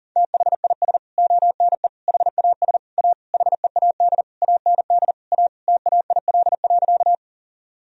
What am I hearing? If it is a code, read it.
Morse code, heard as a keyed tone, about 40 wpm, 700 Hz. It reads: THIS ONE HAS A HEAD AND A TAIL.
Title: Sentences from Top 1000 Words - Review Part 3 - 40wpm